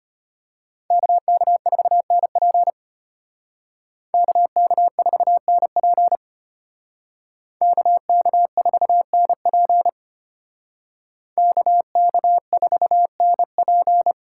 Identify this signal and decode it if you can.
Morse code, heard as a keyed tone, about 33 wpm, 700 Hz. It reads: KK4NP KK4NP KK4NP KK4NP